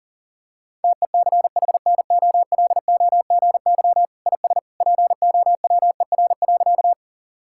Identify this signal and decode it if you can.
Morse code, heard as a keyed tone, about 40 wpm, 700 Hz. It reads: TECHNOLOGY IS POWER.
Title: Sentences from Top 1000 Words - Review Part 5 - 40wpm